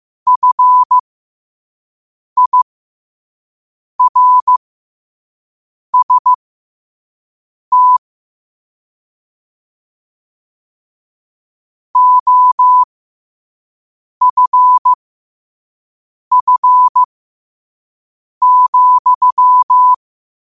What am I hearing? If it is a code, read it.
Morse code, heard as a keyed tone, about 15 wpm, spaced as for 5 wpm, 1000 Hz. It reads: FIRST OFF,
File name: qotdfiles / 5wpm